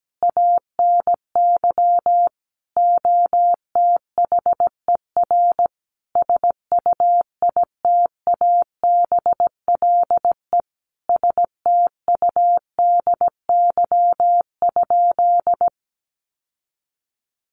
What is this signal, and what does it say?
Morse code, heard as a keyed tone, about 17 wpm, 700 Hz. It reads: ANY OTHER SUITABLE STUDY?
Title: Sentences from Top 1400 Words 17wpm